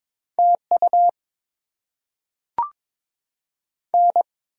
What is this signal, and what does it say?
Morse code, heard as a keyed tone, about 22 wpm, 690 Hz. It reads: TU N
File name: Top 20 QSO Elements - 6x Character Spacing - 22wpm